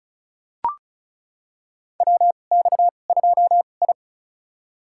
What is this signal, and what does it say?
Morse code, heard as a keyed tone, about 34 wpm, spaced as for 25 wpm, 700 Hz. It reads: WX2I
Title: USA Call Signs - 4 Characters - 2x Character Spacing - 35wpm